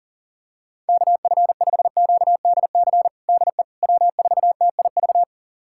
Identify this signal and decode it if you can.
Morse code, heard as a keyed tone, about 40 wpm, 700 Hz. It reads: KF5QDC DE W4TIV
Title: USA Callsign DE Callsign 40wpm